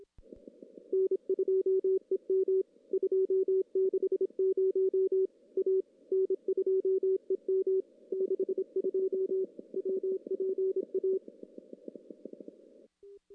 Morse code, heard as a keyed tone, about 27 wpm, 380 Hz. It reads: N2EM 260 A N2EM 62 WPA
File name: CW / CW-SS-k5tr- -1700